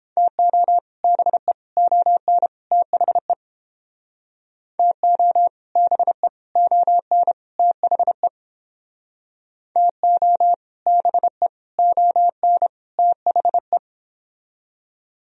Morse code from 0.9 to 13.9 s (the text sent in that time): BE ON THE TO BE ON THE TO BE ON THE